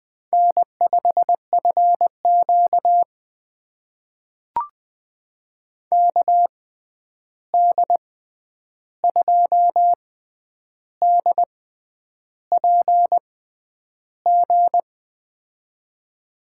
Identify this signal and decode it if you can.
Morse code, heard as a keyed tone, about 20 wpm, spaced as for 9 wpm, 700 Hz. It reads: N5FQ KD2DPG
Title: USA Call Signs - 6x Character Spacing - 20wpm